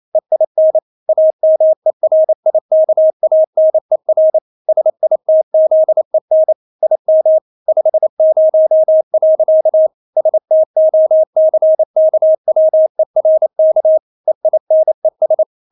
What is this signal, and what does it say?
Morse code, heard as a keyed tone, about 28 wpm, 630 Hz. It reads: EIN AMERIKANER SITZEN IM 50. STOCKWERK EINES
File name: Witz9 28wpm